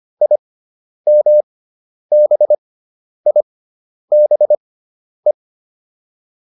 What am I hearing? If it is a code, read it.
Morse code, heard as a keyed tone, about 25 wpm, spaced as for 10 wpm, 600 Hz. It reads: IMBIBE
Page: intermediate-practice-files